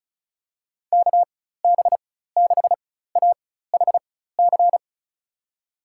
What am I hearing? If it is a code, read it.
Morse code, heard as a keyed tone, about 36 wpm, spaced as for 16 wpm, 700 Hz. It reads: KB6AHC
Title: USA Call Signs - 6 Characters - 4x Character Spacing - 35wpm